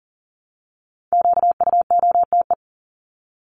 Morse code, heard as a keyed tone, about 40 wpm, 700 Hz. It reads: QUOTE
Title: Top 1751-2000 Words 40wpm